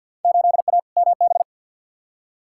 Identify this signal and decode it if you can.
Morse code, heard as a keyed tone, about 53 wpm, 700 Hz. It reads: 8A MB
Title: Field Day Exchanges Encore - 50wpm